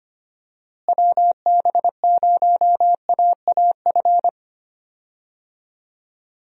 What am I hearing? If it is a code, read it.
Morse code, heard as a keyed tone, about 25 wpm, 700 Hz. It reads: WB0AAF
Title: USA Call Signs - 6 Characters - Repeated 3x - 25wpm